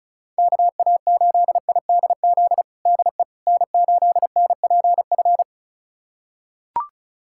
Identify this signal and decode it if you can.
Morse code, heard as a keyed tone, about 36 wpm, 700 Hz. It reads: KA8IDZ DE N8NPF
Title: USA Callsign DE Callsign 35wpm